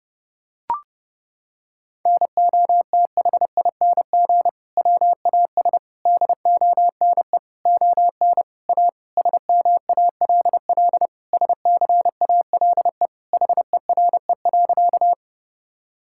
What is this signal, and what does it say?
Morse code, heard as a keyed tone, about 30 wpm, 700 Hz. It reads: NOTHING WAS DONE ON A SMALL SCALE HERE.
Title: Sentences from Top 1000 Words - Review Part 5 - 30wpm